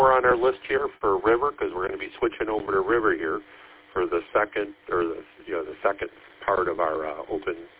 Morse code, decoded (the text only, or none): none